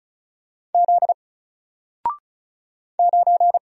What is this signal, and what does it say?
Morse code, heard as a keyed tone, about 33 wpm, 700 Hz. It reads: Z 9
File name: Single-Letter-Number-35wpm